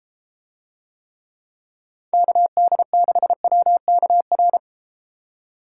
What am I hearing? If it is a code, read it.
Morse code, heard as a keyed tone, about 31 wpm, 700 Hz. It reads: KD6WKR